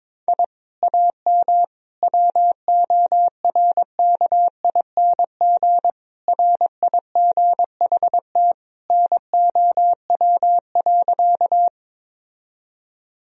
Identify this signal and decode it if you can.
Morse code, heard as a keyed tone, about 22 wpm, 700 Hz. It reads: I AM WORKING RIGHT NOW.
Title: Sentences from Top 1000 Words - Review Part 4 - 22wpm